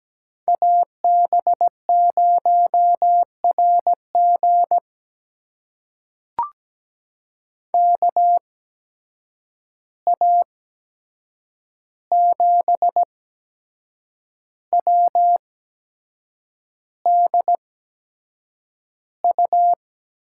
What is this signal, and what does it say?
Morse code, heard as a keyed tone, about 17 wpm, spaced as for 7 wpm, 700 Hz. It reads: AB0RG KA7WDU
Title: USA Call Signs - 8x Character Spacing - 17wpm